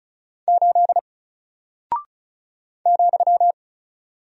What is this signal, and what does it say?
Morse code, heard as a keyed tone, about 36 wpm, 700 Hz. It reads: 8 ,
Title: Single Character (Letter:Number:Punctuation:Prosign) Alt 35wpm